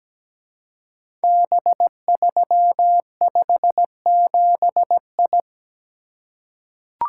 Morse code, heard as a keyed tone, about 17 wpm, 700 Hz. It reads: B357I
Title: Code Groups - 5 Letter:Numbers - 17wpm